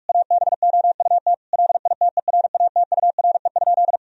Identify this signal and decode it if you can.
Morse code, heard as a keyed tone, about 46 wpm, 700 Hz. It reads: ABOUT LITERATURE?
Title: Sentences from Top 1000 Words - Review Part 3 - 45wpm